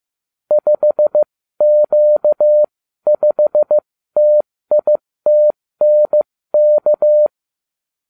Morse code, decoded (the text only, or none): 5Q5TITNK